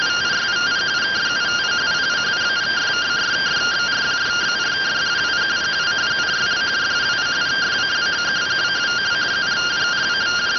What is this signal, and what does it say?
Two-tone teleprinter signal, about 170 Hz shift.